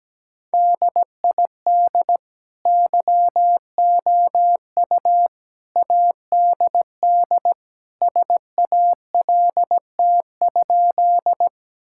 Morse code, heard as a keyed tone, about 17 wpm, 700 Hz. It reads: DID YOU ADD SALT?
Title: Sentences from Top 1000 Words - Review Part 5 - 17wpm